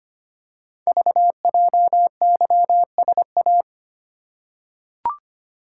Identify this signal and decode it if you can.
Morse code, heard as a keyed tone, about 25 wpm, 690 Hz. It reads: VJYSA
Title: Code Groups - 5 Letters - 25wpm